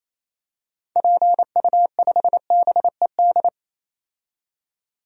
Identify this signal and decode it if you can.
Morse code, heard as a keyed tone, about 28 wpm, 700 Hz. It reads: PU5BED
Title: DXCC-100 Call Signs - 6 Characters - 28wpm